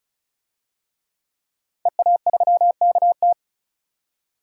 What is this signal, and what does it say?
Morse code, heard as a keyed tone, about 36 wpm, 700 Hz. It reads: EA3KT
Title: DXCC-100 Call Signs - 5 Characters - 35wpm